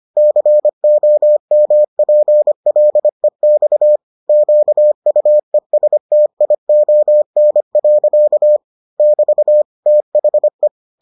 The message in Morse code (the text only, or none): COMPLEX QUESTION. <BT> THE